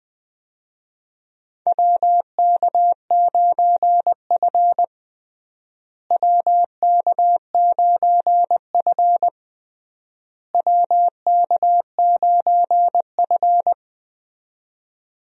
Morse code, heard as a keyed tone, about 20 wpm, 700 Hz. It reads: WK9F WK9F WK9F
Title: USA Call Signs - 4 Characters - Repeated 3x - 20wpm